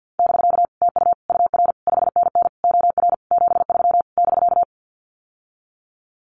Morse code, heard as a keyed tone, about 49 wpm, 700 Hz. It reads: <BK> TU UR 5NN OR 73 <BK>
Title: Common QSO Phrases Encore 50wpm